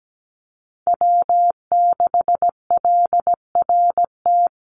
Morse code, heard as a keyed tone, about 17 wpm, 700 Hz. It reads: W6LRT
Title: USA Call Signs 17wpm